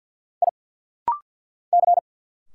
Morse code, heard as a keyed tone, about 50 wpm, 700 Hz. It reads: I C
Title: Single Letters - Rapid-Fire 50wpm